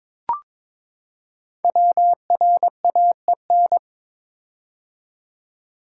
Morse code, heard as a keyed tone, about 23 wpm, 700 Hz. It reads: WRAEN
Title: Code Groups - 5 Letters - 22wpm